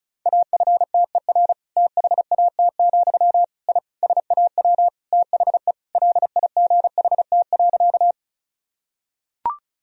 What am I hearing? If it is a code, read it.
Morse code, heard as a keyed tone, about 36 wpm, 700 Hz. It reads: AFTER THAT, I SAW THE LIGHT.
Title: Sentences from Top 1000 Words - Review Part 1 - 35wpm